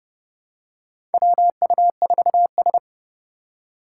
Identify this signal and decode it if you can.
Morse code, heard as a keyed tone, about 30 wpm, 700 Hz. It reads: WU4S
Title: USA Call Signs - 4x Character Spacing - 30wpm